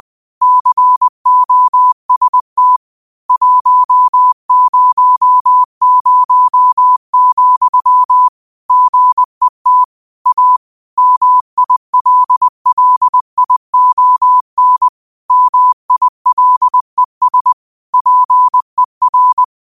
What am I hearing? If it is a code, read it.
Morse code, heard as a keyed tone, about 20 wpm, 1000 Hz. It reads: COST 100, GET A MILLION MILES PER
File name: qotdfiles / 20wpm